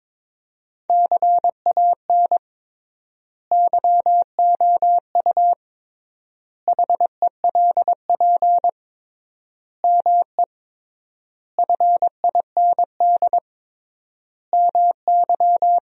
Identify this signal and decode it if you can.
Morse code, heard as a keyed tone, about 21 wpm, 700 Hz. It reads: CAN YOU HELP ME FIND MY
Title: 3x Word-Spacing - Sentences from Top 200 Words - 22wpm